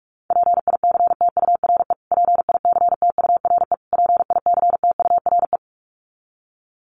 Morse code, heard as a keyed tone, about 44 wpm, 700 Hz. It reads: PICTURE PICTURE PICTURE
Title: Top 500 Words Repeated 45wpm